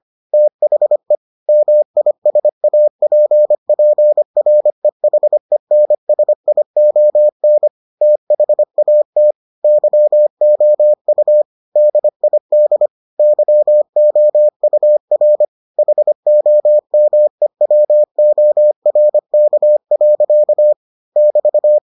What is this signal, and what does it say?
Morse code, heard as a keyed tone, about 25 wpm, 600 Hz. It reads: THE MISAPPREHENSION THAT YOU DID YOUR HOMEWORK. <BT>